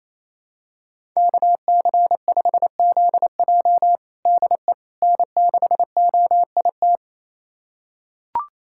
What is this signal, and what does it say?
Morse code, heard as a keyed tone, about 28 wpm, 700 Hz. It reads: KC5ZJ DE N6OIT
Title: USA Callsign DE Callsign - 5 Characters - 28wpm